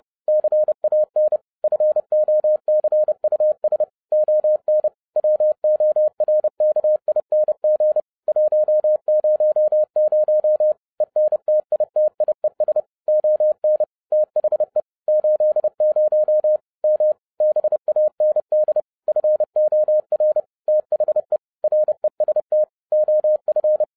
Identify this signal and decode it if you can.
Morse code, heard as a keyed tone, about 30 wpm, 610 Hz. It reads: CAN FOCUS ON WORKING 100 ENTITIES ON THE 80 M BAND FOR THE REST OF
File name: 2012-01-mp3 / n12-I DIDNT MISS THE SOLAR CYCLE 24-30wpm